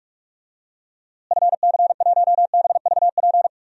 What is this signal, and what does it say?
Morse code, heard as a keyed tone, about 46 wpm, 700 Hz. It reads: FC1BVP
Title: DXCC-100 Call Signs - 6 Characters - Repeated 3x - 45wpm